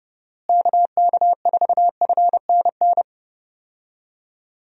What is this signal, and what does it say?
Morse code, heard as a keyed tone, about 30 wpm, 700 Hz. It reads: KK4FNN
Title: USA Call Signs - 6 Characters - Repeated 2x - 30wpm